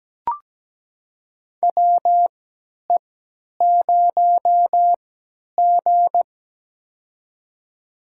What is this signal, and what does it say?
Morse code, heard as a keyed tone, about 17 wpm, spaced as for 10 wpm, 700 Hz. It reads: WE0G
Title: USA Call Signs - 4 Characters - 3x Character Spacing - 17wpm